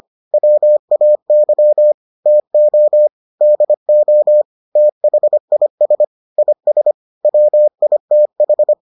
Morse code, heard as a keyed tone, about 25 wpm, 600 Hz. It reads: WAY TO DO THIS IS WITH